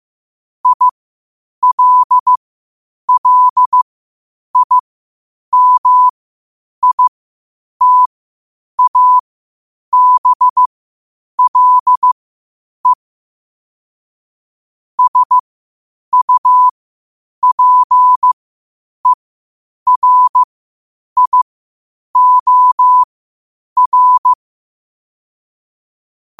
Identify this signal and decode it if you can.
Morse code, heard as a keyed tone, about 15 wpm, spaced as for 8 wpm, 1000 Hz. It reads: ILLIMITABLE SUPERIOR
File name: qotdfiles / 8wpm